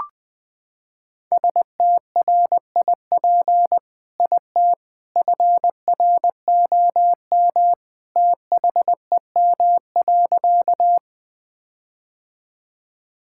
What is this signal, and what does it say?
Morse code, heard as a keyed tone, about 20 wpm, 700 Hz. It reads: STRIP IT FROM THEM.
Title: Sentences from Top 1600 Words 20wpm